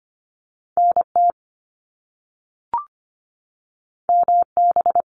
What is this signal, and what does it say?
Morse code, heard as a keyed tone, about 25 wpm, 700 Hz. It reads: NT MB
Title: Canadian Provinces and Territories 25wpm